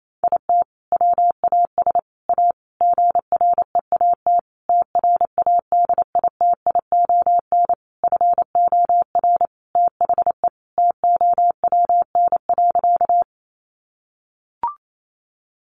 Morse code, heard as a keyed tone, about 28 wpm, 700 Hz. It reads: IT WAS A GREAT TRADITION FOR THE TOWN.
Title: Sentences from Top 1500 Words 28wpm